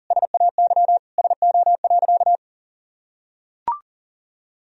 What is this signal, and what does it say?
Morse code, heard as a keyed tone, about 40 wpm, 700 Hz. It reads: SAY SO.
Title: Sentences from Top 1000 Words - Review Part 1 - 40wpm